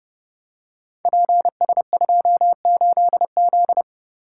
Morse code, heard as a keyed tone, about 30 wpm, 700 Hz. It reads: PS28Z
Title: Code Groups - 5 Letter:Numbers - 30wpm